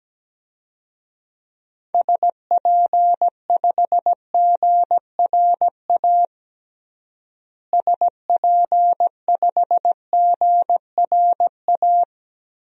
Morse code, heard as a keyed tone, about 17 wpm, 700 Hz. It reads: SP5GRA SP5GRA
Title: DXCC-100 Call Signs - 6 Characters - Repeated 2x - 17wpm